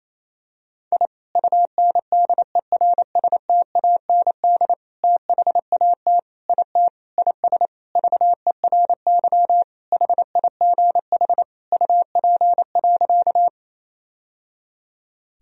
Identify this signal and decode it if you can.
Morse code, heard as a keyed tone, about 27 wpm, 700 Hz. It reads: I UNDERSTAND THAT IT IS VERY HIGH UP.
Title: Sentences from Top 1000 Words - Review Part 1 - 28wpm